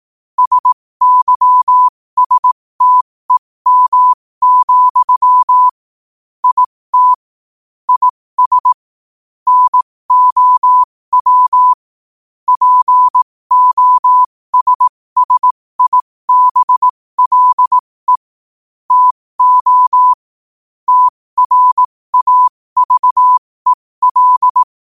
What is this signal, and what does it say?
Morse code, heard as a keyed tone, about 18 wpm, spaced as for 15 wpm, 1000 Hz. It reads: SYSTEM, IT IS NOW POSSIBLE TO TRAVEL
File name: qotdfiles / 15wpm